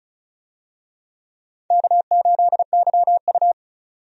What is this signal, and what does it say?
Morse code, heard as a keyed tone, about 34 wpm, 700 Hz. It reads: K8YU
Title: USA Call Signs - 4 Characters - 35wpm